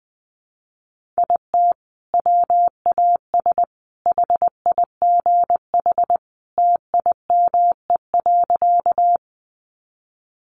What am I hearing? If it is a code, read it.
Morse code, heard as a keyed tone, about 20 wpm, 700 Hz. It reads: IT WAS HIGH TIME.
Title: Sentences from Top 200 Words 20wpm